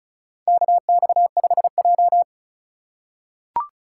Morse code, heard as a keyed tone, about 34 wpm, 700 Hz. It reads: KX5J
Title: USA Call Signs - 4 Characters - 4x Character Spacing - 35wpm